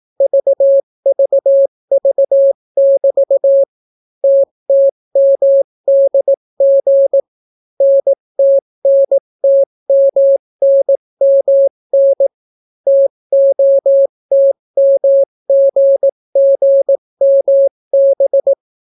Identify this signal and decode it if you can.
Morse code, heard as a keyed tone, about 18 wpm, 550 Hz. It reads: VVV<BT> TTMDG NTNTMNMN TOTMGGMB